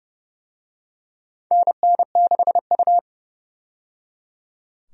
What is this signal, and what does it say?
Morse code, heard as a keyed tone, about 29 wpm, 700 Hz. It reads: NN6U
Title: USA Call Signs - 4 Characters - 30wpm